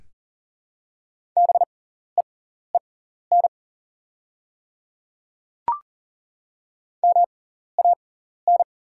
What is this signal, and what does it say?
Morse code, heard as a keyed tone, about 38 wpm, spaced as for 14 wpm, 690 Hz. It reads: BEEN MAN
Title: 6x Character Spacing - Top 200 Words - 40wpm